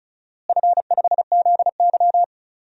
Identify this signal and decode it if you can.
Morse code, heard as a keyed tone, about 36 wpm, 700 Hz. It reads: F5ZY
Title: DXCC-100 Call Signs - 4 Characters - Repeated 2x - 35wpm